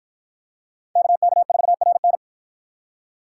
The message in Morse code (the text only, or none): KK4RN